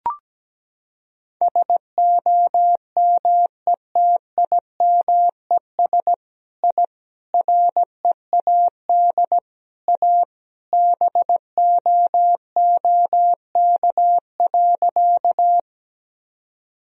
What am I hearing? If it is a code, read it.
Morse code, heard as a keyed tone, about 17 wpm, 700 Hz. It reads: SOMETIMES I READ A BOOK.
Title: Sentences from Top 1000 Words - Review Part 3 - 17wpm